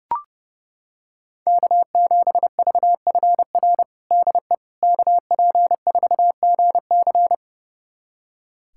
Morse code, heard as a keyed tone, about 32 wpm, 700 Hz. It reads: K7VFR DE KP4GC
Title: USA Callsign DE Callsign - 5 Characters - 30wpm